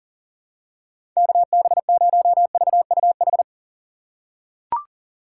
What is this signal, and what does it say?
Morse code, heard as a keyed tone, about 40 wpm, 700 Hz. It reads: KB0VUH
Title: USA Call Signs - 6 Characters - 2x Character Spacing - 40wpm